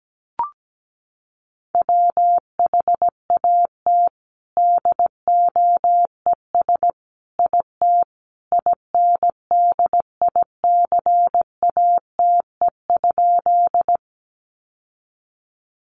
Morse code, heard as a keyed tone, about 17 wpm, 700 Hz. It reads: WHAT DOES IT INDICATE?
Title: Sentences from Top 1300 Words 17wpm